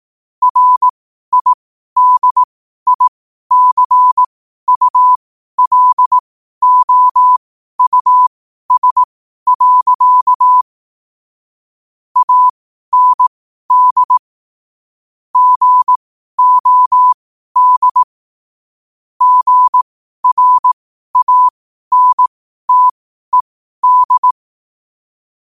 Morse code, heard as a keyed tone, about 18 wpm, spaced as for 12 wpm, 1000 Hz. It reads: RIDICULOUS. AND GOD GRANTED
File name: qotdfiles / 12wpm